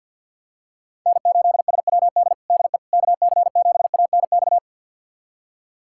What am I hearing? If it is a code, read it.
Morse code, heard as a keyed tone, about 53 wpm, 700 Hz. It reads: N8SWD DE KC7ANX
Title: USA Callsign DE Callsign 50wpm